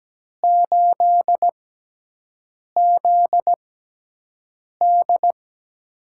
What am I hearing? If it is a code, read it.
Morse code, heard as a keyed tone, about 17 wpm, spaced as for 6 wpm, 700 Hz. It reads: 8ZD
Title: USA Call Signs - 4 Characters - 6x Character Spacing - 17wpm